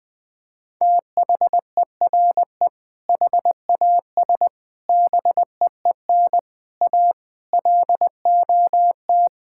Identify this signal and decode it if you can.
Morse code, heard as a keyed tone, about 21 wpm, 700 Hz. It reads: THERE HAS BEEN A LOT
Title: Sentences with 8-Word Phrases - Spoken Beginning - 20wpm